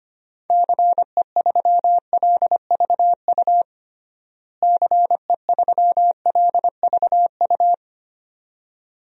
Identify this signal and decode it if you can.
Morse code, heard as a keyed tone, about 25 wpm, 700 Hz. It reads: CE3LVU CE3LVU
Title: DXCC-100 Call Signs - 6 Characters - Repeated 2x - 25wpm